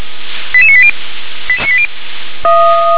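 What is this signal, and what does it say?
Five-tone selective call (ZVEI), heard as quick runs of five tones, tones of about 70 ms. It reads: 88980
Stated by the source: ZVEI five-tone selective call 88631 received on SDR.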